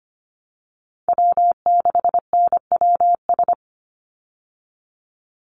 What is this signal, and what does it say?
Morse code, heard as a keyed tone, about 25 wpm, 700 Hz. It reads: W6NWS
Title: Call Signs Repeated 3x - 25wpm